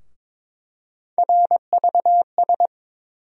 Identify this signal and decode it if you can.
Morse code, heard as a keyed tone, about 22 wpm, 700 Hz. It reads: RVS
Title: USA Call Sign Postfixes - 3 Letters 22wpm